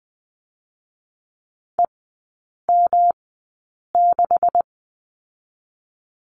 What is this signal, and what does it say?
Morse code, heard as a keyed tone, about 20 wpm, spaced as for 8 wpm, 700 Hz. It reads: EM6
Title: 127 Lesson 21e - Number 6 - Three Characters 20wpm